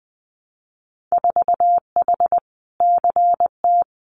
Morse code, heard as a keyed tone, about 20 wpm, 700 Hz. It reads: 4H CT
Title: Winter Field Day Exchanges Encore 20wpm